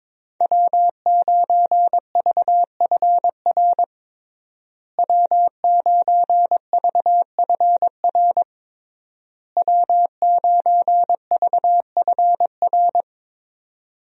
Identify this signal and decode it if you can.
Morse code, heard as a keyed tone, about 22 wpm, 700 Hz. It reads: W9VFR W9VFR W9VFR
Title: USA Call Signs - 5 Characters - Repeated 3x - 22wpm